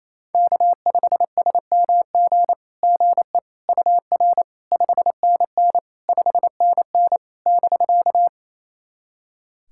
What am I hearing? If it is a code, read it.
Morse code, heard as a keyed tone, about 28 wpm, 700 Hz. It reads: K5SMG GE UR 5NN 5NN <BK>
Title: SOTA QSO Practice 28wpm